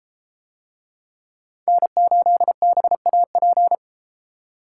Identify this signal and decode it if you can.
Morse code, heard as a keyed tone, about 32 wpm, 700 Hz. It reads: N8BAP